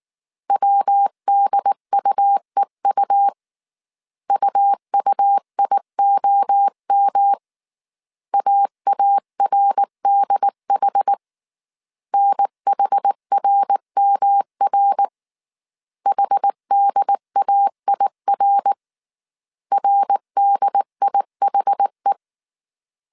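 Morse code, heard as a keyed tone, about 19 wpm, 800 Hz. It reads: WDUEU UUIOM AARDH NHRMR HDAIR RDIHE
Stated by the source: Morse code dictation at 18 wpm from the IW3SGT QRS Morse code training course, five-character groups of 300 characters.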